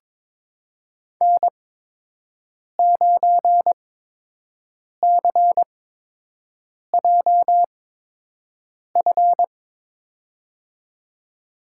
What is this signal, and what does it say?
Morse code, heard as a keyed tone, about 22 wpm, spaced as for 6 wpm, 700 Hz. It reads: N9CJF
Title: USA Call Signs - 5 Characters - 8x Character Spacing - 22wpm